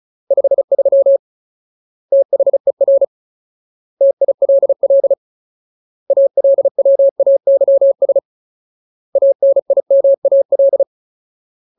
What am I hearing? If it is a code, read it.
Morse code, heard as a keyed tone, about 36 wpm, 550 Hz. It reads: 53 THER TILL ALWAYS ANIMAL
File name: Ultimate Word List / 35WPM Ultimate